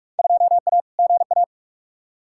Morse code, heard as a keyed tone, about 47 wpm, 700 Hz. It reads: 2A GA
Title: Field Day Exchanges - 45wpm